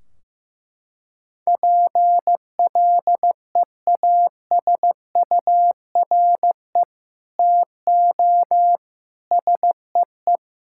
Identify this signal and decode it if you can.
Morse code, heard as a keyed tone, about 16 wpm, 700 Hz. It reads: PLEASURE TO SEE
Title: Sentences from Top 1600 Words 15wpm